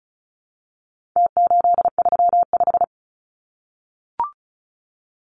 Morse code, heard as a keyed tone, about 34 wpm, 700 Hz. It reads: T835
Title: Numbers - Contest Style - 4-Digit 35wpm